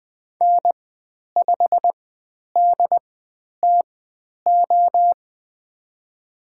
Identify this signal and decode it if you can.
Morse code, heard as a keyed tone, about 20 wpm, spaced as for 10 wpm, 700 Hz. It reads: N5DTO
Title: 095 Lesson 16g - Letter c - Call Signs 20wpm